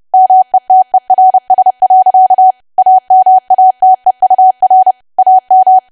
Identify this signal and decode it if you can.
Morse code, heard as a keyed tone, about 30 wpm, 750 Hz. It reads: METERS. AMATEUR AM
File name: cwa / W1AW-Level3 / 30WPM